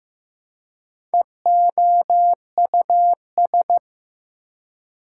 Morse code, heard as a keyed tone, about 15 wpm, 700 Hz. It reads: EOUS
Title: Prefixes and Suffixes 15wpm